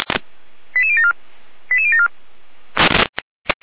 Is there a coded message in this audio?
Yes: five-tone selective call (ZVEI) from 0.8 to 2.1 s, heard as quick runs of five tones, tones of about 70 ms. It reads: 88074 88074